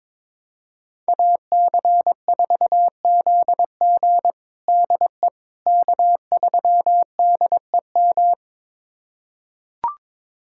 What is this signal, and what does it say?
Morse code, heard as a keyed tone, about 22 wpm, 700 Hz. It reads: AC4ZG DE K3DEM
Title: USA Callsign DE Callsign - 5 Characters - 22wpm